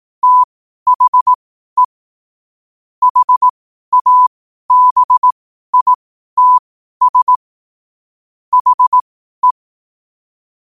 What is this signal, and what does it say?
Morse code, heard as a keyed tone, about 18 wpm, spaced as for 12 wpm, 1000 Hz. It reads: THE HABITS HE
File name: qotdfiles / 12wpm